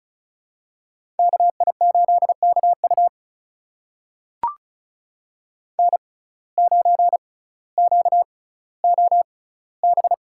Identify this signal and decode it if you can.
Morse code, heard as a keyed tone, about 36 wpm, spaced as for 18 wpm, 700 Hz. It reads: KI8KU N9QOB
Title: USA Call Signs - 5 Characters - 6x Character Spacing - 35wpm